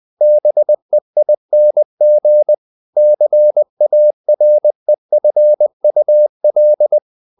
Morse code, heard as a keyed tone, about 20 wpm, 600 Hz. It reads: BEING CAREFUL
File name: the old man and the sea / the old man and the sea part 1 20wpm